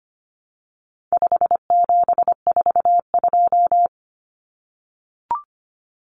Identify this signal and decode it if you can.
Morse code, heard as a keyed tone, about 25 wpm, 700 Hz. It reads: 5742
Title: Common Cut Numbers - 4-Digit 25wpm